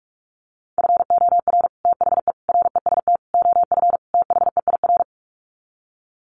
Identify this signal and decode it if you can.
Morse code, heard as a keyed tone, about 44 wpm, 700 Hz. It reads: FOR THE REST OF THEIR
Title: Sets of 5 Words Encore2 45wpm